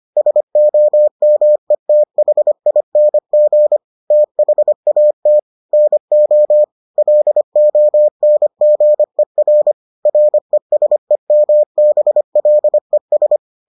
Morse code, heard as a keyed tone, about 25 wpm, 600 Hz. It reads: SOMETHING THAT NO LONGER RESEMBLES